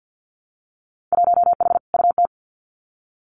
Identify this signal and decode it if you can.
Morse code, heard as a keyed tone, about 50 wpm, 700 Hz. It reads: 1H UT
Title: Winter Field Day Exchanges Encore 50wpm